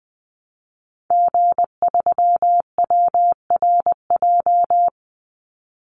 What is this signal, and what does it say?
Morse code, heard as a keyed tone, about 20 wpm, 700 Hz. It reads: G3WRJ
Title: Call Signs - 20wpm